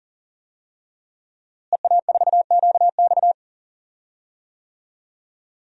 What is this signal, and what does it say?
Morse code, heard as a keyed tone, about 40 wpm, 700 Hz. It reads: EA4QX
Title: DXCC-100 Call Signs - 5 Characters - Repeated 3x - 40wpm